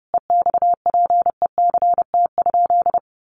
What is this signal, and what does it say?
Morse code, heard as a keyed tone, about 28 wpm, 700 Hz. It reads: EXPECT?
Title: Sentences from Top 1100 Words 30wpm